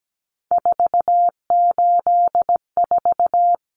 Morse code, heard as a keyed tone, about 17 wpm, 700 Hz. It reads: 484
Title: Numbers - 3-Digit 17wpm